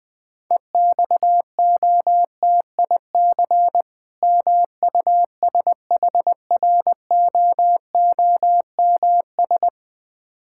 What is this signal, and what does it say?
Morse code, heard as a keyed tone, about 20 wpm, 700 Hz. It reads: EXOTIC MUSHROOMS
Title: Difficult Words with Hints - 20wpm